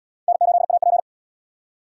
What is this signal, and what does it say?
Morse code, heard as a keyed tone, about 64 wpm, 700 Hz. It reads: N8AP